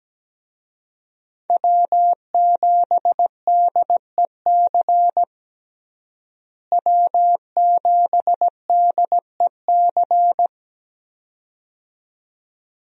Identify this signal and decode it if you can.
Morse code, heard as a keyed tone, about 17 wpm, 700 Hz. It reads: W7DEC W7DEC
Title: USA Call Signs - 5 Characters - Repeated 2x - 17wpm